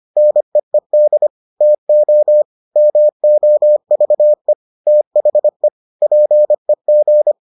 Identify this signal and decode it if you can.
Morse code, heard as a keyed tone, about 25 wpm, 600 Hz. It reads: NEED TO MOVE THE PEG